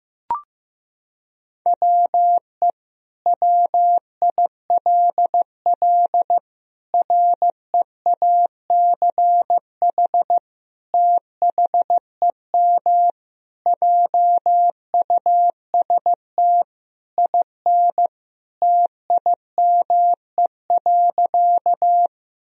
Morse code, heard as a keyed tone, about 15 wpm, 700 Hz. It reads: WE WILL REACH THEM JUST IN TIME.
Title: Sentences from Top 1000 Words - Review Part 2 - 15wpm